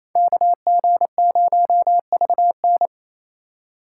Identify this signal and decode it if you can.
Morse code, heard as a keyed tone, about 35 wpm, 700 Hz. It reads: KG0VN